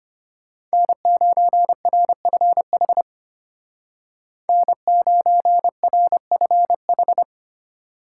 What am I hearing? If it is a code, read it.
Morse code, heard as a keyed tone, about 33 wpm, 700 Hz. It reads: N9RFH N9RFH